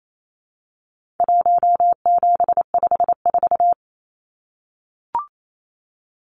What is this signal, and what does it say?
Morse code, heard as a keyed tone, about 28 wpm, 700 Hz. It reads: 1754
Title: Numbers - Contest Style - 4-Digit 28wpm